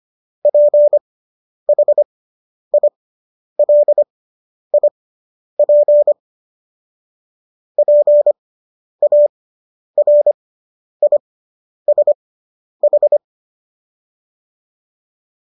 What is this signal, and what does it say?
Morse code, heard as a keyed tone, about 25 wpm, spaced as for 10 wpm, 590 Hz. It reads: PHILIP PARISH